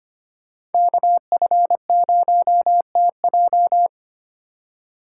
Morse code, heard as a keyed tone, about 25 wpm, 700 Hz. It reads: KF0TJ
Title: USA Call Signs - 5 Characters - 6x Character Spacing - 25wpm